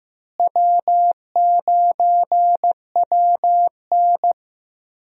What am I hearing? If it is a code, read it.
Morse code, heard as a keyed tone, about 15 wpm, 700 Hz. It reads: W9WN
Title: USA Call Signs - 4 Characters - Repeated 3x - 15wpm